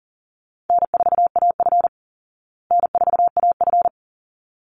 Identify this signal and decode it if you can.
Morse code, heard as a keyed tone, about 40 wpm, 700 Hz. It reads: N4AF N4AF
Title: Easy Call Signs Repeated 2x - 40wpm